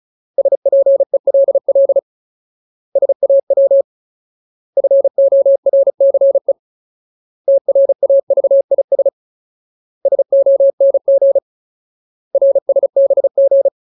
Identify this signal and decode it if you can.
Morse code, heard as a keyed tone, about 34 wpm, 550 Hz. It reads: SPELL SAW FORCE TRAVIS SONG RSBG
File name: Ultimate Word List / 35WPM Ultimate